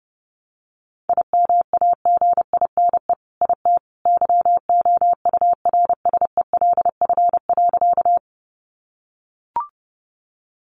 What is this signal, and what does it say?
Morse code, heard as a keyed tone, about 30 wpm, 700 Hz. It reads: IMAGINE IT YOURSELF.
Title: Sentences from Top 1400 Words 30wpm